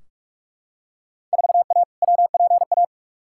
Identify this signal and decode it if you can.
Morse code, heard as a keyed tone, about 44 wpm, 700 Hz. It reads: 4A WPA
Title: Field Day Exchanges Encore - 45wpm